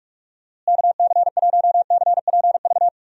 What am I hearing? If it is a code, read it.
Morse code, heard as a keyed tone, about 45 wpm, 700 Hz. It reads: KC1CPV
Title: USA Call Signs - 6 Characters - Repeated 3x - 45wpm